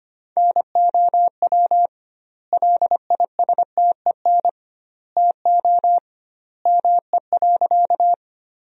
Morse code, heard as a keyed tone, about 25 wpm, 700 Hz. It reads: NOW LISTEN TO ME.
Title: 2x Word-Spacing - Sentences from Top 400 Words - 25wpm